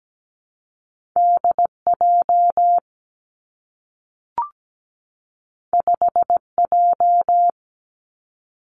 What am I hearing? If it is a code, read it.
Morse code, heard as a keyed tone, about 17 wpm, 700 Hz. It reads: DJ 5J
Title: Two Challenging Letter-Number Combinations 17wpm